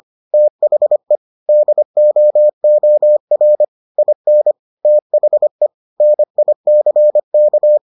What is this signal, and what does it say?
Morse code, heard as a keyed tone, about 25 wpm, 600 Hz. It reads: THE DOOR IN THE NICK